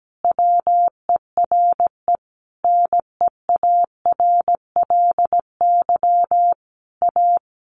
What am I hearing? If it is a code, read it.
Morse code, heard as a keyed tone, about 17 wpm, 700 Hz. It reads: WERE NEARLY A
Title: Sentences from Top 1300 Words 17wpm